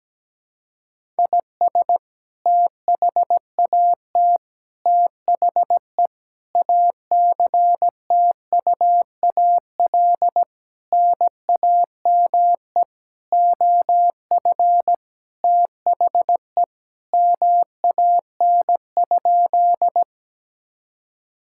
Morse code, heard as a keyed tone, about 17 wpm, 700 Hz. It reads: IS THAT THE ACTUAL NAME OF THE MAN?
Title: Sentences from Top 1000 Words - Review Part 5 - 17wpm